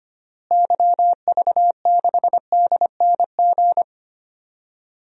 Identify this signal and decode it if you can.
Morse code, heard as a keyed tone, about 25 wpm, 700 Hz. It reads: YV6DNG
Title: DXCC-100 Call Signs - 6 Characters - Repeated 2x - 25wpm